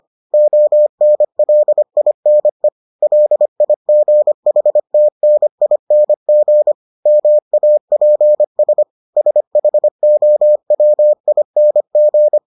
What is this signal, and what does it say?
Morse code, heard as a keyed tone, about 25 wpm, 600 Hz. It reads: ONLINE LIGHTNING MAPS SHOWING